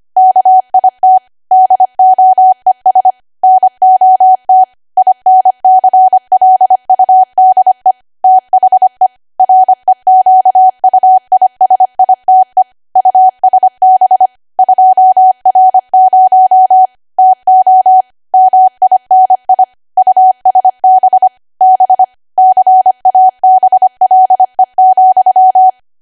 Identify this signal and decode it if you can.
Morse code, heard as a keyed tone, about 25 wpm, 750 Hz. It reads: KIT DOES NOT INCLUDE THE REQUISITE USB 2R0 TO MINI USB B CABLE,
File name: cwa / W1AW-Level3 / 25WPM